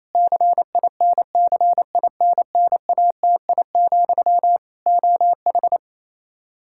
Morse code, heard as a keyed tone, about 28 wpm, 700 Hz. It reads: CINCINNATI, OH
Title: K1USN SST QSO Practice 28wpm